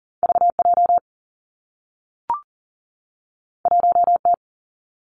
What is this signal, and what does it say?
Morse code, heard as a keyed tone, about 41 wpm, 700 Hz. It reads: V J 1 T
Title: Two Challenging Letter-Number Combinations 40wpm